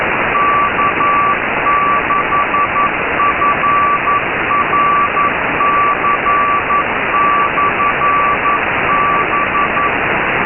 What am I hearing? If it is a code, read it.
Morse code, heard as a keyed tone, about 11 wpm, 1160 Hz, with noise. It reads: K6FRC/B
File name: Radio / SDR Audio / gqrx